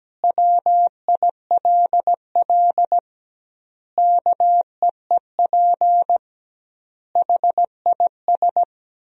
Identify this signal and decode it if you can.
Morse code, heard as a keyed tone, about 17 wpm, 700 Hz. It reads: WILL KEEP HIS
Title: 2x Word-Spacing - Sentences from Top 300 Words - 17wpm